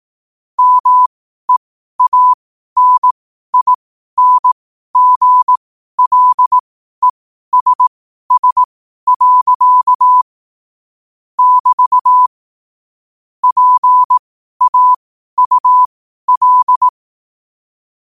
Morse code, heard as a keyed tone, about 18 wpm, spaced as for 12 wpm, 1000 Hz. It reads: MEANINGLESS. <BT> PAUL
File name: qotdfiles / 12wpm